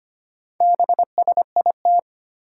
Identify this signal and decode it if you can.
Morse code, heard as a keyed tone, about 25 wpm, 700 Hz. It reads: BSIT
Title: Four Challenging Letter-Number Combinations 25wpm